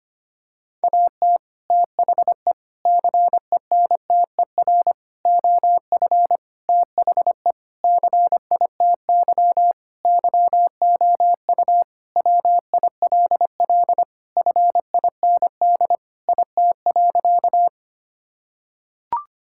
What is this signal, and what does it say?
Morse code, heard as a keyed tone, about 25 wpm, 700 Hz. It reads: AT THE CENTER OF THE CITY YOU WILL FIND IT.
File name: practice / 25wpm / Sentences with 4-Word Phrases - 25wpm